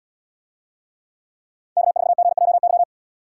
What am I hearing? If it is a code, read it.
Morse code, heard as a keyed tone, about 74 wpm, 700 Hz. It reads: K4GJY